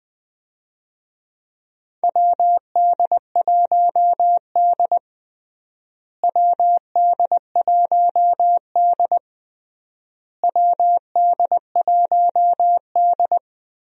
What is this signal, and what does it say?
Morse code, heard as a keyed tone, about 20 wpm, 700 Hz. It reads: WD1D WD1D WD1D
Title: USA Call Signs - 4 Characters - Repeated 3x - 20wpm